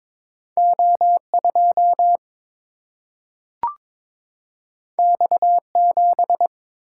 Morse code, heard as a keyed tone, about 22 wpm, 700 Hz. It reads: O 2 X 7
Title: Single-Letter Single-Number 22wpm